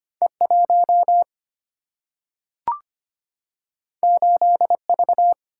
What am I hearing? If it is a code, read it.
Morse code, heard as a keyed tone, about 25 wpm, 700 Hz. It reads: E1 8V
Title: Single-Letter Single-Number 25wpm